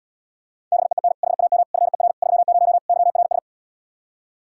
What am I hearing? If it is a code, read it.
Morse code, heard as a keyed tone, about 77 wpm, 700 Hz. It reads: BEEN HAM FER 30 YRS